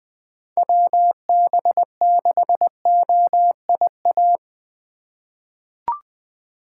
Morse code, heard as a keyed tone, about 20 wpm, 700 Hz. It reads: WB6OIA
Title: USA Call Signs - 2x Character Spacing - 20wpm